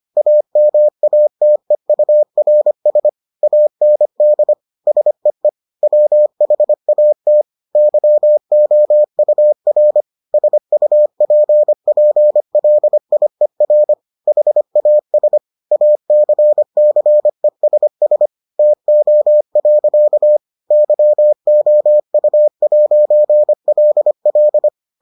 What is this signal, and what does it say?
Morse code, heard as a keyed tone, about 25 wpm, 600 Hz. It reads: AMATEURS AND SEE WHAT YOUR SUPPLIER HAS ACCESS TO. YOU'LL